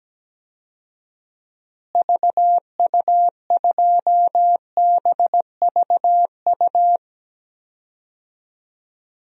Morse code, heard as a keyed tone, about 17 wpm, 700 Hz. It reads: VU2BVU
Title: DXCC-100 Call Signs - 6 Characters - Repeated 3x - 17wpm